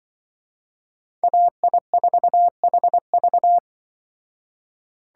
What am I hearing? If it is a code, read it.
Morse code, heard as a keyed tone, about 24 wpm, 700 Hz. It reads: AI4HV
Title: Speed Racing - USA 2x2 Call Signs 20wpm